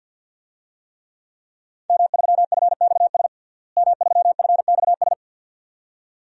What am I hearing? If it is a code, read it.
Morse code, heard as a keyed tone, about 49 wpm, 700 Hz. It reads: M3FXS M3FXS
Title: DXCC-100 Call Signs - 5 Characters - Repeated 2x - 50wpm